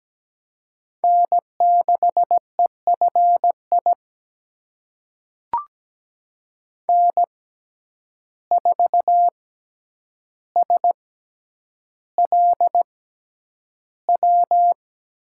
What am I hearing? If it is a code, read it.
Morse code, heard as a keyed tone, about 17 wpm, spaced as for 9 wpm, 700 Hz. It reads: N6EFI N4SLW
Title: USA Call Signs - 5 Characters - 6x Character Spacing - 17wpm